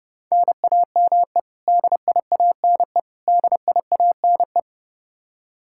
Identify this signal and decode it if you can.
Morse code, heard as a keyed tone, about 32 wpm, 700 Hz. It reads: NAME DIANE DIANE
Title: Practice QSO - Exchange 1 - Beginning - 30wpm